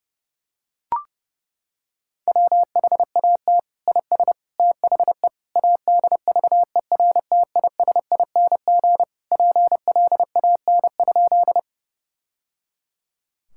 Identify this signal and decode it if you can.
Morse code, heard as a keyed tone, about 30 wpm, 700 Hz. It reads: WHAT IS THE ADVERTISING PLAN?
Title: Sentences from Top 1000 Words - Review Part 6 - 30wpm